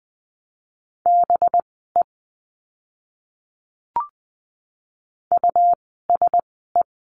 Morse code, heard as a keyed tone, about 20 wpm, spaced as for 15 wpm, 700 Hz. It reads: BE USE
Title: 192 Lesson 31f - Letter v - Words 20wpm